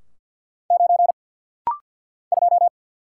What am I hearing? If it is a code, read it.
Morse code, heard as a keyed tone, about 50 wpm, 690 Hz. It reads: 9 2
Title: Single Letter-Number Alt - Rapid-Fire 50wpm